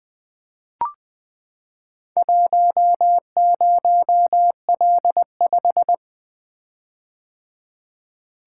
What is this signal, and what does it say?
Morse code, heard as a keyed tone, about 20 wpm, 700 Hz. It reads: 10L5
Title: Four Letter-Number Combinations - 20wpm